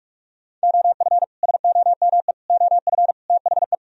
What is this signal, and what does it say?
Morse code, heard as a keyed tone, about 58 wpm, spaced as for 51 wpm, 700 Hz. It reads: OF SOME OF THE
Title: Speed-Racing - Sets of 4 Words 45wpm